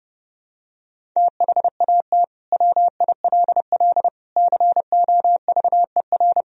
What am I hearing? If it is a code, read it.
Morse code, heard as a keyed tone, about 30 wpm, 700 Hz. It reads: THAT WILL COVER
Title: Sentences from Top 1000 Words - Review Part 1 - 30wpm